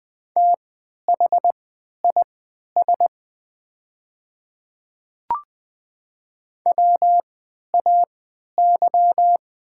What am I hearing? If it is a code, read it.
Morse code, heard as a keyed tone, about 20 wpm, spaced as for 11 wpm, 700 Hz. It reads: THIS WAY
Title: 3x Character Spacing - Top 100 Words - 20wpm